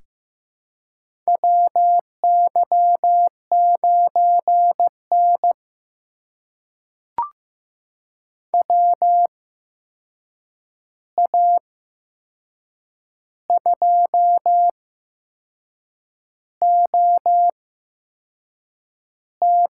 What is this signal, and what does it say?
Morse code, heard as a keyed tone, about 15 wpm, spaced as for 6 wpm, 700 Hz. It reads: WY9N WA2OT
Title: USA Call Signs - 8x Character Spacing - 15wpm